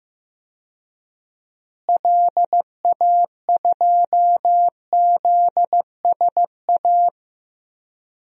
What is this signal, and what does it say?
Morse code, heard as a keyed tone, about 16 wpm, 700 Hz. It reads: LA2ZSA
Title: DXCC-100 Call Signs - 6 Characters - Repeated 2x - 15wpm